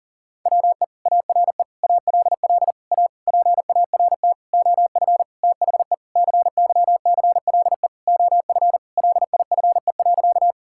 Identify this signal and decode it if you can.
Morse code, heard as a keyed tone, about 40 wpm, 690 Hz. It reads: WE ARE ALL A PART OF THE CYCLE OF LIFE.
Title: Sentences from Top 1000 Words - Review Part 6 - 40wpm